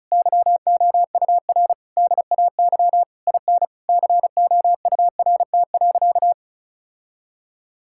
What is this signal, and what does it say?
Morse code, heard as a keyed tone, about 36 wpm, 700 Hz. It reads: YOUR DAY IN COURT.
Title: Sentences from Top 1600 Words 35wpm